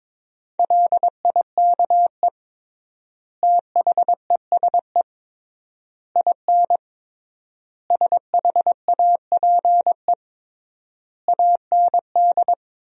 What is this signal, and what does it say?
Morse code, heard as a keyed tone, about 22 wpm, 700 Hz. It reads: LIKE THESE IN SHAPE AND
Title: 3x Word-Spacing - Sentences from Top 500 Words - 22wpm